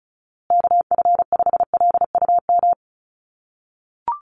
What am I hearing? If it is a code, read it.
Morse code, heard as a keyed tone, about 34 wpm, 700 Hz. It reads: KF5LUM
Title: USA Call Signs Repeated 2x 35wpm